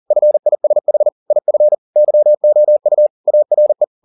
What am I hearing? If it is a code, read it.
Morse code, heard as a keyed tone, about 40 wpm, 600 Hz. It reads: FISH IF YOU ARE